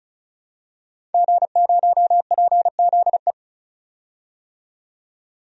Morse code, heard as a keyed tone, about 36 wpm, 700 Hz. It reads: G0PZE
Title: DXCC-100 Call Signs - 5 Characters - 35wpm